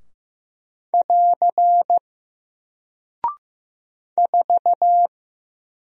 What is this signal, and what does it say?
Morse code, heard as a keyed tone, about 15 wpm, 700 Hz. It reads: <AR> 4
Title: Single Character (Letter:Number:Punctuation:Prosign) 15wpm